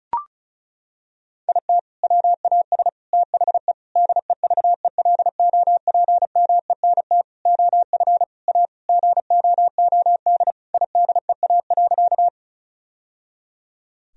Morse code, heard as a keyed tone, about 36 wpm, 700 Hz. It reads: IT WAS THE DEVELOPMENT OF A GOOD IDEA.
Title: Sentences from Top 1000 Words - Review Part 4 - 35wpm